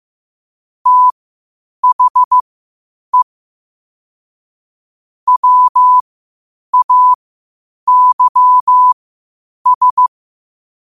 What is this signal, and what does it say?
Morse code, heard as a keyed tone, about 15 wpm, spaced as for 8 wpm, 1000 Hz. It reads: THE WAYS
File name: qotdfiles / 8wpm